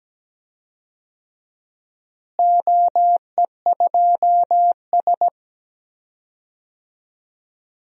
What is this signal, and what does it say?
Morse code, heard as a keyed tone, about 17 wpm, 700 Hz. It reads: OE2S
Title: DXCC-100 Call Signs - 4 Characters - Repeated 3x - 17wpm